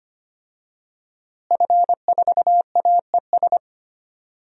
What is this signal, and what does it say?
Morse code, heard as a keyed tone, about 25 wpm, 700 Hz. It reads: F4AES
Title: DXCC-100 Call Signs - 5 Characters - Repeated 2x - 25wpm